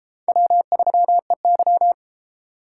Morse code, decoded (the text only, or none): W3EY